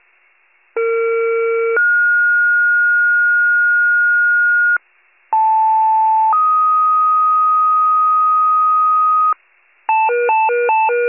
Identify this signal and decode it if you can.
Two-tone page, heard as a one-second tone, then a three-second tone, about 460 then 1430 Hz, about 880 then 1250 Hz, with noise. It is heard from 0.8 to 9.3 s.
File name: pager / 76 FIRE / wcalert